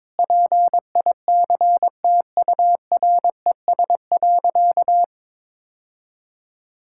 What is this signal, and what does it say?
Morse code, heard as a keyed tone, about 22 wpm, 700 Hz. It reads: PICTURES.
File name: practice / 22wpm / Sentences with 8-Word Phrases - 22wpm